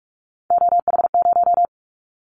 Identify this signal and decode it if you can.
Morse code, heard as a keyed tone, about 46 wpm, 700 Hz. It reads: OH0
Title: Callsign Prefixes for Top 100 DXCC - 45wpm